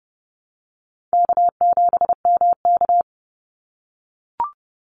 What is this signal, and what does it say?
Morse code, heard as a keyed tone, about 30 wpm, 700 Hz. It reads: K7MK
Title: NA SOTA Activator Callsigns 30wpm